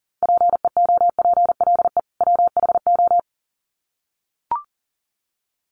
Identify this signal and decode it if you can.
Morse code, heard as a keyed tone, about 40 wpm, 700 Hz. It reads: PEOPLE WHO
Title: Sets of 2 Words 40wpm